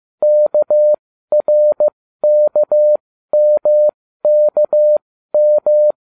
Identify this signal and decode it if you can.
Morse code, heard as a keyed tone, about 15 wpm, spaced as for 12 wpm, 600 Hz. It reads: KRKMKM